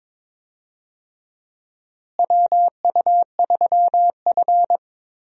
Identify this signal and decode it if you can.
Morse code, heard as a keyed tone, about 22 wpm, 700 Hz. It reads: WU3F
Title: USA Call Signs - 4 Characters - Repeated 3x - 22wpm